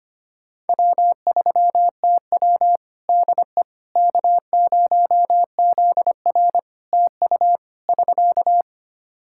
Morse code, heard as a keyed tone, about 25 wpm, 700 Hz. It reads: W3TW DE K0ZR TU <SK>
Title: K1USN SST QSO Practice 25wpm